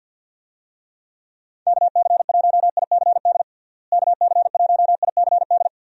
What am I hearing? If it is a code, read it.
Morse code, heard as a keyed tone, about 50 wpm, 700 Hz. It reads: KC1ICD KC1ICD
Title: USA Call Signs - 6 Characters - Repeated 2x - 50wpm